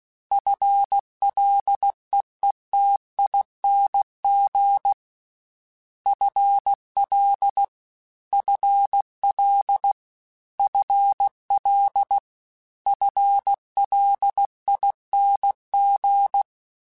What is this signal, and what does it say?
Morse code, heard as a keyed tone, about 16 wpm, 800 Hz. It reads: FLEETING FL FL FL FLING
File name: FL Prefixes 3x2 @15